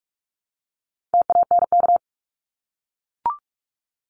Morse code, heard as a keyed tone, about 44 wpm, 700 Hz. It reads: TANK
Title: Top 1751-2000 Words 45wpm